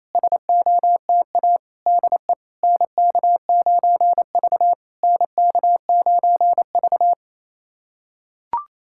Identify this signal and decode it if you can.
Morse code, heard as a keyed tone, about 28 wpm, 700 Hz. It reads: SOTA DE NK9V NK9V
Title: SOTA QSO Practice 28wpm